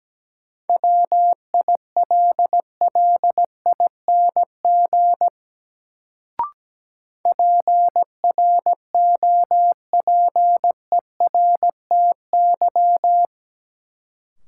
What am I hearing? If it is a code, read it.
Morse code, heard as a keyed tone, about 17 wpm, 700 Hz. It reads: WILLING PROPERTY
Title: Sentences from Top 1000 Words - New Word Drill - 17wpm